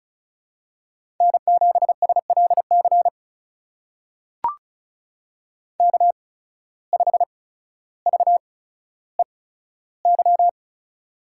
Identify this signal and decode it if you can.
Morse code, heard as a keyed tone, about 36 wpm, spaced as for 15 wpm, 700 Hz. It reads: N7SLC K5VEY
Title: USA Call Signs - 5 Characters - 8x Character Spacing - 35wpm